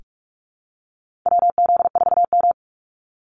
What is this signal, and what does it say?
Morse code, heard as a keyed tone, about 44 wpm, 700 Hz. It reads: WZ4M
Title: NA SOTA Activator Callsigns 45wpm